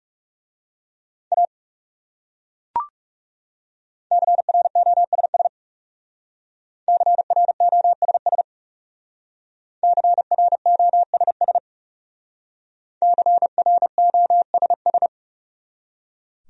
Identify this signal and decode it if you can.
Morse code, heard as a keyed tone, about 38 wpm, 700 Hz. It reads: A CROSS CROSS CROSS CROSS